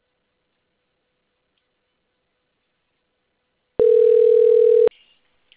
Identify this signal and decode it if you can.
Ringback tone (North America), 440 + 480 Hz.